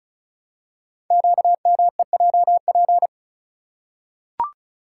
Morse code, heard as a keyed tone, about 36 wpm, 700 Hz. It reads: QMEJP
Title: Code Groups - 5 Letter:Numbers - 35wpm